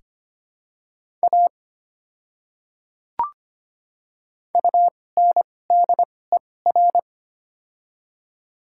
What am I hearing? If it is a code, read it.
Morse code, heard as a keyed tone, about 25 wpm, spaced as for 18 wpm, 700 Hz. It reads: A UNDER
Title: 2x Character Spacing - Top 200 Words - 25wpm